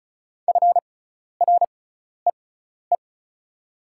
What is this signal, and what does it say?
Morse code, heard as a keyed tone, about 34 wpm, spaced as for 12 wpm, 690 Hz. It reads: FREE
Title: 6x Character Spacing - Top 500 Words - 35wpm